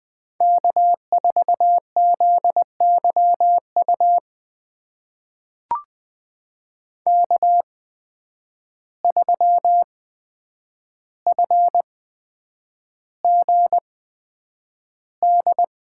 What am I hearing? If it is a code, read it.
Morse code, heard as a keyed tone, about 20 wpm, spaced as for 9 wpm, 700 Hz. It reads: K4ZYU K3FGD
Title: USA Call Signs - 5 Characters - 8x Character Spacing - 20wpm